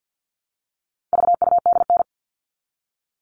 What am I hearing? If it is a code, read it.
Morse code, heard as a keyed tone, about 50 wpm, 700 Hz. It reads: VUDN
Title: All Cut Numbers - 4-Digit 50wpm